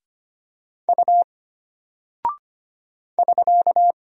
Morse code, heard as a keyed tone, about 25 wpm, 700 Hz. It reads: U <SK>
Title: Single Character (Letter:Number:Punctuation:Prosign) Alt 25wpm